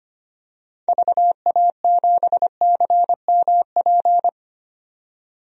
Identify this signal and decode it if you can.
Morse code, heard as a keyed tone, about 25 wpm, 700 Hz. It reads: VA7CMP
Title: DXCC-100 Call Signs - 6 Characters - 25wpm